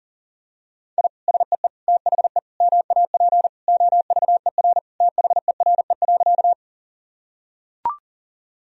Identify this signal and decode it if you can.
Morse code, heard as a keyed tone, about 40 wpm, 700 Hz. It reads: I SEE THE MAP OVER THERE.
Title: Sentences from Top 1000 Words - Review Part 2 - 40wpm